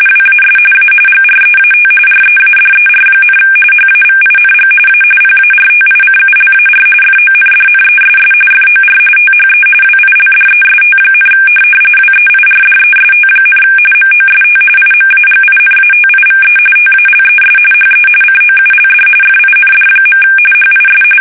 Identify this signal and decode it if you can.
RTTY, about 45.45 baud. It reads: WAS A GREAT READER, HAVING A PARTICULAR FONDNESS FOR THE WILDEST, WEIRDEST, MOST IMAGINATIVE TALES HE COULD FIND. HIS
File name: fsk850 / benjamin-g-lamme